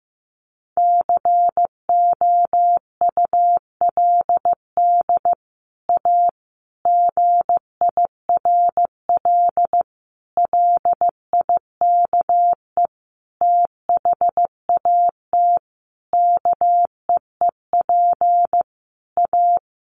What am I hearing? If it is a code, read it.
Morse code, heard as a keyed tone, about 15 wpm, 700 Hz. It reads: COULD A GIRL LIKE THAT KEEP A
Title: Sentences from Top 1200 Words 15wpm